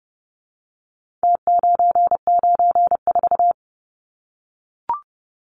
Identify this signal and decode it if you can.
Morse code, heard as a keyed tone, about 30 wpm, 700 Hz. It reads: T994
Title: Numbers - Contest Style - 4-Digit 30wpm